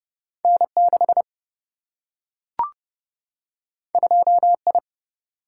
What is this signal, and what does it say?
Morse code, heard as a keyed tone, about 30 wpm, 700 Hz. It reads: N6 2I
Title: Single-Letter Single-Number 30wpm